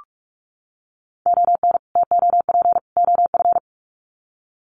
Morse code, heard as a keyed tone, about 44 wpm, 700 Hz. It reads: ON TOP OF
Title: Sets of 3 Words 45wpm